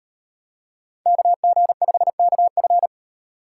Morse code, heard as a keyed tone, about 37 wpm, 700 Hz. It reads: KG5KF